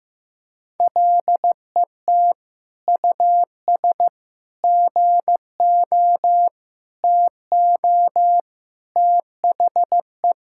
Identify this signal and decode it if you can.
Morse code, heard as a keyed tone, about 15 wpm, 700 Hz. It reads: LET US GO TO THE
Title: Sentences from Top 1000 Words - Review Part 5 - 15wpm